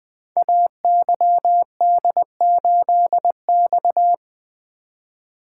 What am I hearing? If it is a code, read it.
Morse code, heard as a keyed tone, about 20 wpm, 700 Hz. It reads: AYD8X
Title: Code Groups - 5 Letter:Numbers - 20wpm